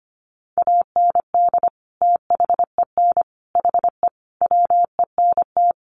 Morse code, heard as a keyed tone, about 25 wpm, 700 Hz. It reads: AND THEN HE WENT
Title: Sentences from Top 1100 Words 25wpm